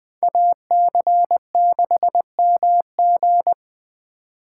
Morse code, heard as a keyed tone, about 21 wpm, 700 Hz. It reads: AC6MG
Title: USA Call Signs - 5 Characters - 3x Character Spacing - 20wpm